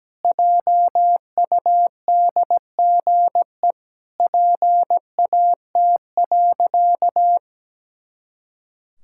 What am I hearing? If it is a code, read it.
Morse code, heard as a keyed tone, about 17 wpm, 700 Hz. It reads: JUDGE PAT.
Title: Pangram Challenge v2 - 17wpm